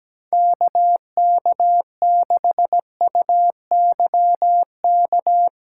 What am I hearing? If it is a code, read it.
Morse code, heard as a keyed tone, about 17 wpm, 700 Hz. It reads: KK6UYK
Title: USA Call Signs - 2x Character Spacing - 17wpm